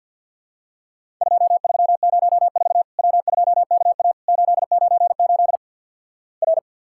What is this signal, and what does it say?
Morse code, heard as a keyed tone, about 50 wpm, 700 Hz. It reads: 2304 W2KA 898 R
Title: Worked All Europe DX Contest - QTC Exchange - 50wpm